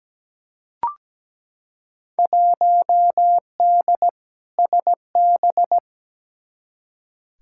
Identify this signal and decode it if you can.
Morse code, heard as a keyed tone, about 17 wpm, 700 Hz. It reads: 1D SB
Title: Field Day Exchanges - 17wpm